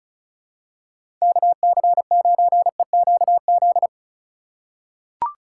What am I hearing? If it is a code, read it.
Morse code, heard as a keyed tone, about 34 wpm, 700 Hz. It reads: KC9EQZ
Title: USA Call Signs - 6 Characters - 4x Character Spacing - 35wpm